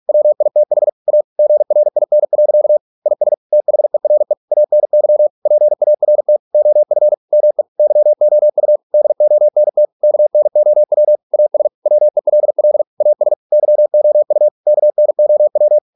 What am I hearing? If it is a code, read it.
Morse code, heard as a keyed tone, about 46 wpm, 600 Hz. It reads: WITH A GRIN. IS THERE ANY PART OF ME YOU DONT KNOW AS WELL AS YOU KNOW